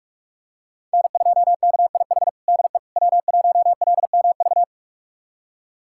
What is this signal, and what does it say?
Morse code, heard as a keyed tone, about 45 wpm, 700 Hz. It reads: N2KIH DE W1LMV
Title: USA Callsign DE Callsign - 5 Characters - 45wpm